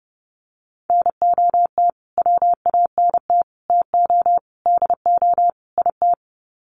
Morse code, heard as a keyed tone, about 30 wpm, 700 Hz. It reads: NOT WANT TO DO IT
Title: Sets of 5 Words Encore 30wpm